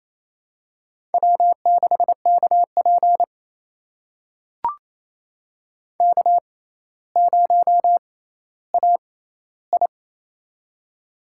Morse code, heard as a keyed tone, about 28 wpm, spaced as for 14 wpm, 700 Hz. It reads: W6KP K0AI
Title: USA Call Signs - 4 Characters - 6x Character Spacing - 28wpm